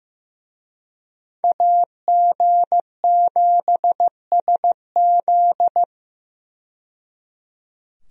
Morse code, heard as a keyed tone, about 15 wpm, 700 Hz. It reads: AG7SZ
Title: USA Call Signs - 5 Characters - 15wpm